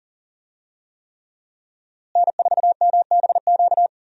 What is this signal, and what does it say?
Morse code, heard as a keyed tone, about 40 wpm, 700 Hz. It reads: N4MBQ
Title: USA Call Signs - 5 Characters - Repeated 3x - 40wpm